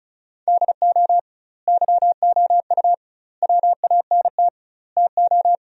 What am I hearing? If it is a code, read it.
Morse code, heard as a keyed tone, about 36 wpm, 700 Hz. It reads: DO YOU WANT TO
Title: 2x Word-Spacing - Sentences from Top 300 Words - 35wpm